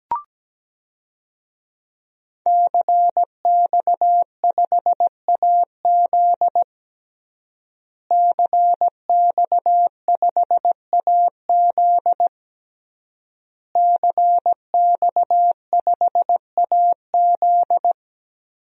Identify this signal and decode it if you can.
Morse code, heard as a keyed tone, about 17 wpm, 700 Hz. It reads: CX5AZ CX5AZ CX5AZ
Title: DXCC-100 Call Signs - 5 Characters - Repeated 3x - 17wpm